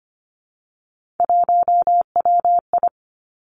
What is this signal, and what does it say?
Morse code, heard as a keyed tone, about 25 wpm, 700 Hz. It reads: 1WI
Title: Three Challenging Letter-Number Combinations 25wpm